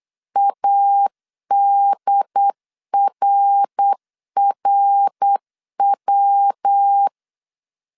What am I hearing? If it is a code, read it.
Morse code, heard as a keyed tone, about 8 wpm, 800 Hz. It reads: ADRRW
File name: CW / DET / 1 3 1500C S B 8wpm 4141